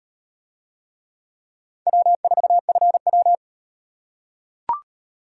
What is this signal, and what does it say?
Morse code, heard as a keyed tone, about 36 wpm, 700 Hz. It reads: W4FW